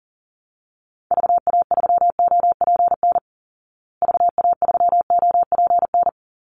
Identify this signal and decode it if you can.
Morse code, heard as a keyed tone, about 40 wpm, 700 Hz. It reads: VA3OPN VA3OPN
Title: Call Signs Repeated 3x - 40wpm